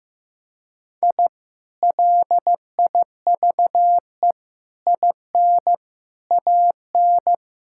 Morse code, heard as a keyed tone, about 15 wpm, 700 Hz. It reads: I LIVE IN AN
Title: Sentences from Top 1000 Words - Review Part 6 - 15wpm